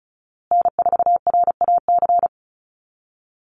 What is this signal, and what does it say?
Morse code, heard as a keyed tone, about 34 wpm, 700 Hz. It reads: N4RAC
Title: USA Call Signs Repeated 2x 35wpm